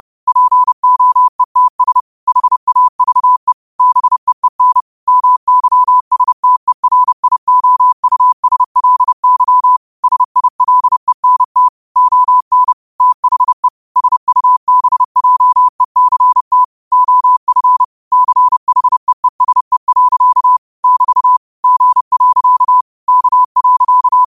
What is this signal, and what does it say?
Morse code, heard as a keyed tone, about 30 wpm, 1000 Hz. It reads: POETS HAVE BEEN MYSTERIOUSLY SILENT ON THE SUBJECT OF CHEESE. <BT> G. K.
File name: qotdfiles / 30wpm